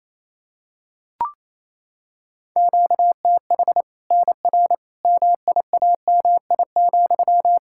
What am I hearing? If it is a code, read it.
Morse code, heard as a keyed tone, about 28 wpm, 700 Hz. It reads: QTH NR MIAMI,
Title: Practice QSO - Exchange 1 - Beginning - 28wpm